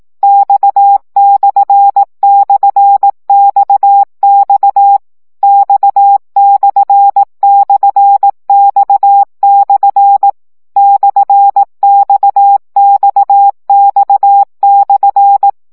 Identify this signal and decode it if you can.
Morse code, heard as a keyed tone, about 18 wpm, 800 Hz. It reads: X//XX X//X/ /XXX/
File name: CW gyakorlas x per 15 baud